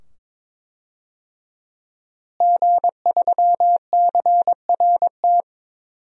Morse code, heard as a keyed tone, about 22 wpm, 700 Hz. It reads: G3CRT
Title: DXCC-100 Call Signs - 5 Characters - Repeated 2x - 22wpm